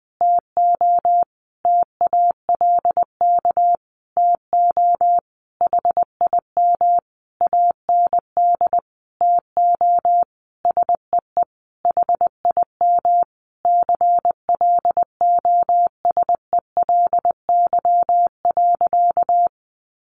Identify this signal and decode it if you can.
Morse code, heard as a keyed tone, about 20 wpm, 700 Hz. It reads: TO TALK TO HIM AND TO SEE HIM CLOSELY.
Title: Sentences from Top 1400 Words 20wpm